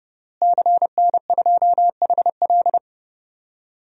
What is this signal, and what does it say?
Morse code, heard as a keyed tone, about 30 wpm, 700 Hz. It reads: CN2HL
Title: DXCC-100 Call Signs - 5 Characters - Repeated 3x - 30wpm